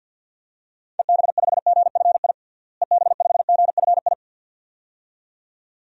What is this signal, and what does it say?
Morse code, heard as a keyed tone, about 48 wpm, 700 Hz. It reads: EB5GFI EB5GFI
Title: DXCC-100 Call Signs - 6 Characters - Repeated 2x - 50wpm